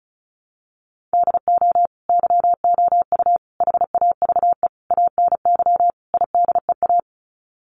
Morse code, heard as a keyed tone, about 34 wpm, 700 Hz. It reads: DO YOU HAVE ANY IDEA
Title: Sets of 5 Words Encore 35wpm